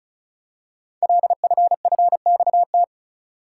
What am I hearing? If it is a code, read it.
Morse code, heard as a keyed tone, about 36 wpm, 690 Hz. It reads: LFFXT
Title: Code Groups - 5 Letters - 35wpm